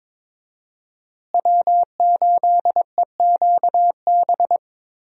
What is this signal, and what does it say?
Morse code, heard as a keyed tone, about 22 wpm, 700 Hz. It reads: W8EQB
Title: USA Call Signs - 5 Characters - Repeated 2x - 22wpm